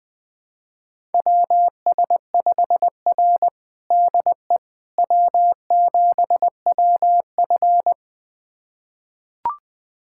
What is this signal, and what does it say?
Morse code, heard as a keyed tone, about 21 wpm, 700 Hz. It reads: WS5R DE W7WF
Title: USA Callsign DE Callsign - 4 Characters - 20wpm